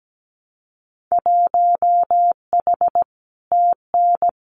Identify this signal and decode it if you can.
Morse code, heard as a keyed tone, about 17 wpm, 700 Hz. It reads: 1H TN
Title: Winter Field Day Exchanges 17wpm